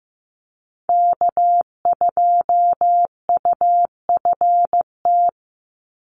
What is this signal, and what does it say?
Morse code, heard as a keyed tone, about 15 wpm, 700 Hz. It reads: K2UFT
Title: Call Signs Repeated 3x - 15wpm